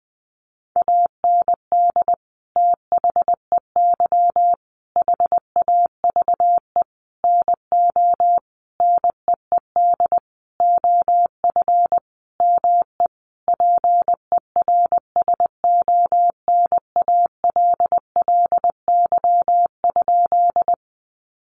Morse code, heard as a keyed tone, about 20 wpm, 700 Hz. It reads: AND THEY HAVE NO NEED OF ME PERSONALLY?
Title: Sentences from Top 1400 Words 20wpm